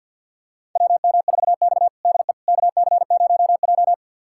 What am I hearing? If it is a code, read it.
Morse code, heard as a keyed tone, about 49 wpm, 700 Hz. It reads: WM4X DE KC0J
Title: USA Callsign DE Callsign - 4 Characters - 50wpm